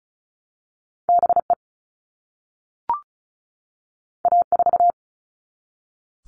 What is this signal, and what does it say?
Morse code, heard as a keyed tone, about 36 wpm, 700 Hz. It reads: BE A4
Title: All Cut Numbers - 2-Digit 35wpm